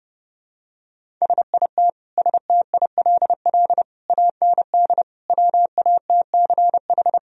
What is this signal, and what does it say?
Morse code, heard as a keyed tone, about 32 wpm, 700 Hz. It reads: SIT STILL AND WATCH
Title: Sentences from Top 1000 Words - Review Part 1 - 30wpm